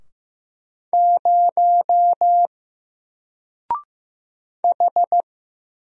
Morse code, heard as a keyed tone, about 15 wpm, 700 Hz. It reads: 0 H
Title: Single Letter-Number Alt 15wpm